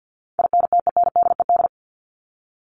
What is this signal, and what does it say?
Morse code, heard as a keyed tone, about 50 wpm, 700 Hz. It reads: INTENDED
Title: ED Suffix - Top 100 Words with Base Words 50wpm